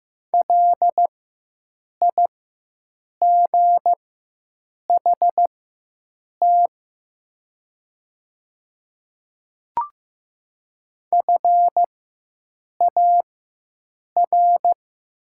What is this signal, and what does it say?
Morse code, heard as a keyed tone, about 15 wpm, spaced as for 7 wpm, 700 Hz. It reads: LIGHT FAR
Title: 4x Character Spacing - Top 500 Words - 15wpm